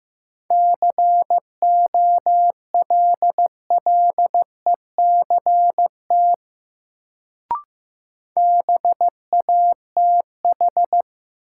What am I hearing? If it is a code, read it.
Morse code, heard as a keyed tone, about 15 wpm, 700 Hz. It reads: COLLECT BATH
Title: Sentences from Top 1600 Words - New Word Drill - 15wpm